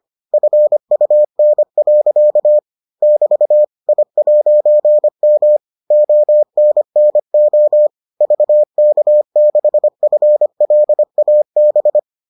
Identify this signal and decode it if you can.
Morse code, heard as a keyed tone, about 25 wpm, 600 Hz. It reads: FUN. <BT> I'M ONNO VK6FLAB